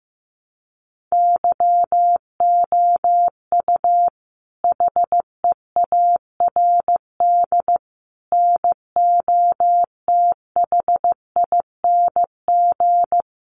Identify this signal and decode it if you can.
Morse code, heard as a keyed tone, about 14 wpm, 700 Hz. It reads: YOU HEARD NOTHING
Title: Sentences from Top 600 Words 15wpm